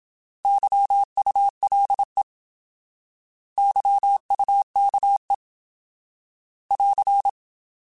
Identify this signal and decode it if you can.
Morse code, heard as a keyed tone, about 27 wpm, 800 Hz. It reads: YULE YUKE <AR>
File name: Alphabetical / Alphabetical mp3s / Ys @25